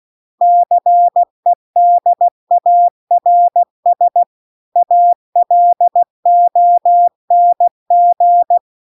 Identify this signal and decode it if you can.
Morse code, heard as a keyed tone, about 16 wpm, 700 Hz. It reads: CEDARS ALONG
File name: CW-The-Big-Twoheaded-River-2-16WPM